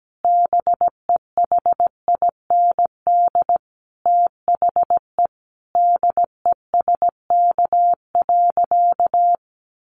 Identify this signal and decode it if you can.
Morse code, heard as a keyed tone, about 17 wpm, 700 Hz. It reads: BEHIND THE DESK.
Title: A Sound Of Thunder 17wpm